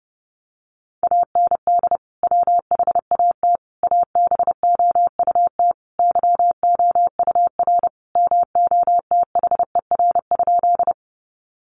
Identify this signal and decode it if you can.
Morse code, heard as a keyed tone, about 30 wpm, 700 Hz. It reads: AND WHAT ABOUT YOUR MOTHER?
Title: Sentences from Top 200 Words 30wpm